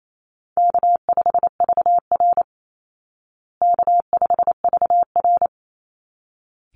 Morse code, heard as a keyed tone, about 28 wpm, 700 Hz. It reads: K5VR K5VR
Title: Call Signs Repeated 3x - 28wpm